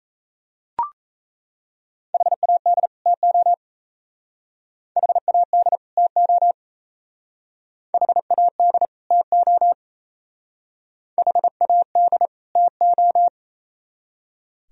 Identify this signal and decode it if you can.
Morse code, heard as a keyed tone, about 34 wpm, 700 Hz. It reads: HAD TO HAD TO HAD TO HAD TO
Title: Speed-Racing - Sets of 2 Words 28wpm